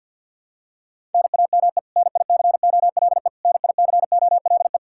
Morse code, heard as a keyed tone, about 48 wpm, 700 Hz. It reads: NAME NICOLE NICOLE
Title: Practice QSO - Exchange 1 - Beginning - 50wpm